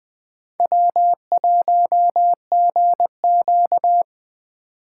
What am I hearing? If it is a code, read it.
Morse code, heard as a keyed tone, about 20 wpm, 700 Hz. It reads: W1GQ